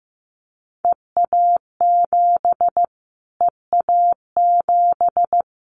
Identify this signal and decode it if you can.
Morse code, heard as a keyed tone, about 15 wpm, 700 Hz. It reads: EA7 EA7
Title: Callsign Prefixes for Top 25 DXCC - Repeated 3x - 15wpm